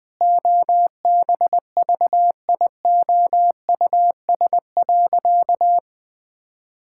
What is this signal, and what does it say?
Morse code, heard as a keyed tone, about 21 wpm, 700 Hz. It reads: OBVIOUS.
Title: Sentences from Top 1600 Words 20wpm